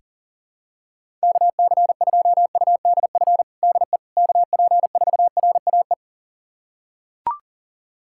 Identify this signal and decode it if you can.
Morse code, heard as a keyed tone, about 37 wpm, 700 Hz. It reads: KC2UDF DE KP4RAE
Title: USA Callsign DE Callsign - 6 Characters - 40wpm